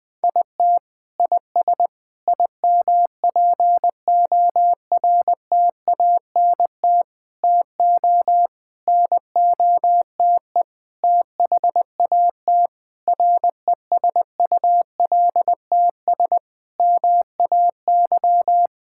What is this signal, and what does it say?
Morse code, heard as a keyed tone, about 21 wpm, 700 Hz. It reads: IT IS IMPORTANT TO NOTE THAT RESULTS MAY
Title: Sentences with 6-Word Phrases - 20wpm